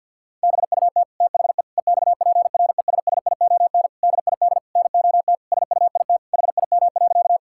Morse code, heard as a keyed tone, about 50 wpm, 700 Hz. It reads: BUT THE EXPRESSION DID NOT SUIT HIM.
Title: Sentences from Top 1600 Words 50wpm